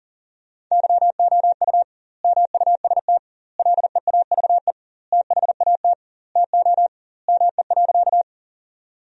Morse code, heard as a keyed tone, about 40 wpm, 700 Hz. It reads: YOU MUST LEAVE THAT TO ME.
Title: 2x Word-Spacing - Sentences from Top 400 Words - 40wpm